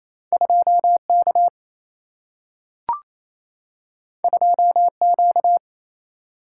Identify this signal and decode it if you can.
Morse code, heard as a keyed tone, about 28 wpm, 700 Hz. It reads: 2K 2Q
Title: Single-Letter Single-Number 28wpm